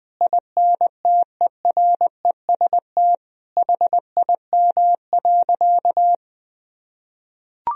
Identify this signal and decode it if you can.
Morse code, heard as a keyed tone, about 21 wpm, 700 Hz. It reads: INTEREST HIM.
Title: Sentences from Top 1000 Words - Review Part 2 - 20wpm